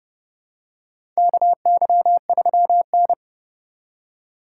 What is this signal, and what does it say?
Morse code, heard as a keyed tone, about 30 wpm, 700 Hz. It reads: KY3N
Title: USA Call Signs - 4 Characters - 2x Character Spacing - 30wpm